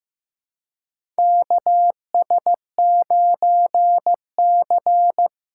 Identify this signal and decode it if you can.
Morse code, heard as a keyed tone, about 15 wpm, 700 Hz. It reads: KS9C
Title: USA Call Signs - 2x Character Spacing - 15wpm